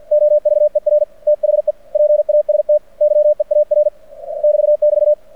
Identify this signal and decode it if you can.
Morse code, heard as a keyed tone, about 48 wpm, 600 Hz. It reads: OVER THE PAST YEAR <BT><BT>
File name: archived / CRSnet